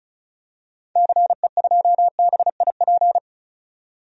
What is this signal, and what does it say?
Morse code, heard as a keyed tone, about 36 wpm, 700 Hz. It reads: CE2BIP
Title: DXCC-100 Call Signs - 6 Characters - 35wpm